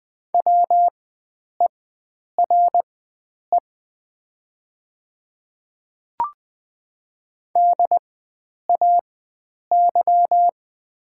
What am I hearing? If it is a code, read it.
Morse code, heard as a keyed tone, about 20 wpm, spaced as for 9 wpm, 700 Hz. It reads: WERE DAY
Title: 4x Character Spacing - Top 100 Words - 20wpm